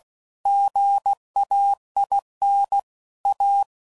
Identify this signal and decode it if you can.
Morse code, heard as a keyed tone, about 16 wpm, 800 Hz. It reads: GAIN A
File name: CW Code Groups @15WPM/Technical Subjects